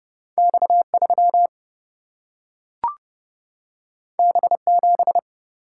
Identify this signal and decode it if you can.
Morse code, heard as a keyed tone, about 31 wpm, 700 Hz. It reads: X 3 B 7
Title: Single-Letter Single-Number 30wpm